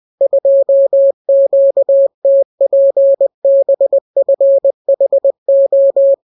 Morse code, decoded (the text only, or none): 2QTPBFHO